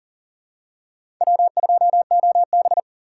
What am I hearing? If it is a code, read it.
Morse code, heard as a keyed tone, about 40 wpm, 700 Hz. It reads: W2OB
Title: USA Call Signs - 6x Character Spacing - 40wpm